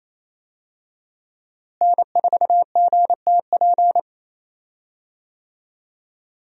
Morse code, heard as a keyed tone, about 28 wpm, 700 Hz. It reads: N4GTP
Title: USA Call Signs - 5 Characters - Repeated 2x - 28wpm